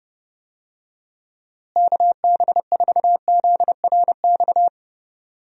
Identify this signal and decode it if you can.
Morse code, heard as a keyed tone, about 30 wpm, 700 Hz. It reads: KB4ZRX
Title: USA Call Signs - 6 Characters - Repeated 2x - 30wpm